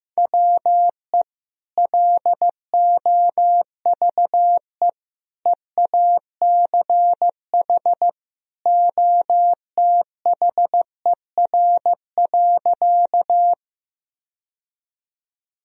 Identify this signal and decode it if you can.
Morse code, heard as a keyed tone, about 16 wpm, 700 Hz. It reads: WE LOVE EACH OTHER.
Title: Sentences from Top 1000 Words - Review Part 1 - 15wpm